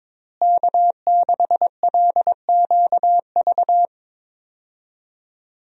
Morse code, heard as a keyed tone, about 22 wpm, 700 Hz. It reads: K6LQV
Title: USA Call Signs - 5 Characters - Repeated 2x - 22wpm